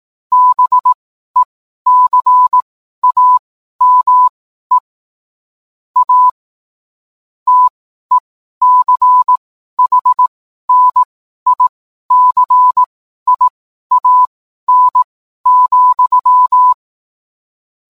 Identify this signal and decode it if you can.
Morse code, heard as a keyed tone, about 18 wpm, spaced as for 12 wpm, 1000 Hz. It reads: BECAME A TECHNICIAN,